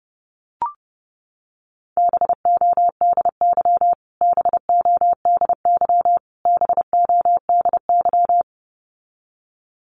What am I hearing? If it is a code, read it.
Morse code, heard as a keyed tone, about 30 wpm, 700 Hz. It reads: BODY BODY BODY
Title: Top 500 Words Repeated 30wpm